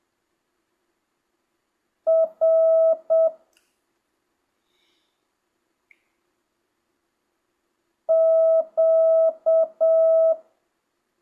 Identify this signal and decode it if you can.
Morse code, heard as a keyed tone, about 7 wpm, 650 Hz. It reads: R Q
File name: jh8cbh / 2025cw